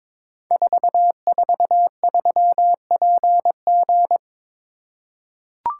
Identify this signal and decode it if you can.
Morse code, heard as a keyed tone, about 22 wpm, 700 Hz. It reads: 443PG
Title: Code Groups - 5 Letter:Numbers - 22wpm